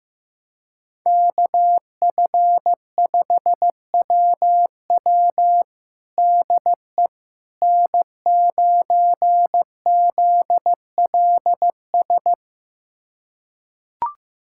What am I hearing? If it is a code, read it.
Morse code, heard as a keyed tone, about 15 wpm, 700 Hz. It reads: KF5WW DE N9ZLS
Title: USA Callsign DE Callsign - 5 Characters - 15wpm